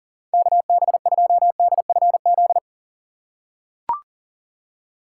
Morse code, heard as a keyed tone, about 40 wpm, 700 Hz. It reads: KB2DFZ
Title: USA Call Signs - 6 Characters - 3x Character Spacing - 40wpm